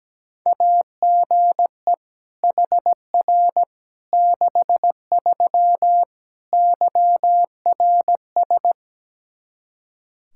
Morse code, heard as a keyed tone, about 17 wpm, 700 Hz. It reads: AGE HR 63 YRS
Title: Practice QSO - Exchange 3 - More Info - 17wpm